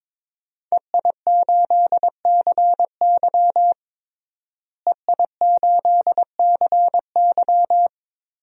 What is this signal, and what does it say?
Morse code, heard as a keyed tone, about 22 wpm, 700 Hz. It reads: EI8CY EI8CY
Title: DXCC-100 Call Signs - 5 Characters - Repeated 3x - 22wpm